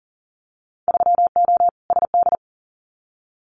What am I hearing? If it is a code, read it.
Morse code, heard as a keyed tone, about 40 wpm, 700 Hz. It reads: 3O SD
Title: Winter Field Day Exchanges 40wpm